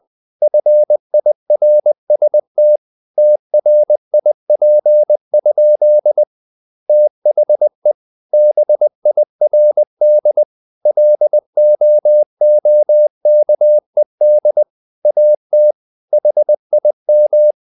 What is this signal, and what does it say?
Morse code, heard as a keyed tone, about 20 wpm, 600 Hz. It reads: FIRST TRIP? THE BIRD LOOKED AT HIM
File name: the old man and the sea / the old man and the sea part 1 20wpm